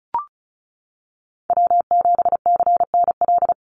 Morse code, heard as a keyed tone, about 34 wpm, 700 Hz. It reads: W7CNL
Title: NA SOTA Chaser Callsigns 35wpm